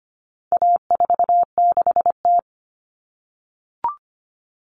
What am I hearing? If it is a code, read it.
Morse code, heard as a keyed tone, about 25 wpm, 700 Hz. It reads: A46T
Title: Common Cut Numbers - 4-Digit 25wpm